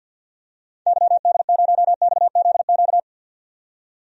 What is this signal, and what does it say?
Morse code, heard as a keyed tone, about 48 wpm, 700 Hz. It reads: YD0XZQ
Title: DXCC-100 Call Signs - 6 Characters - 50wpm